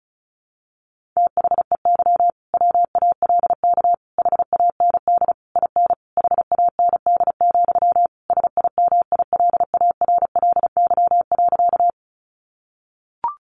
Morse code, heard as a keyed tone, about 34 wpm, 700 Hz. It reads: THEY WALK HAND IN HAND, SIMILARLY.
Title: Sentences from Top 1400 Words 35wpm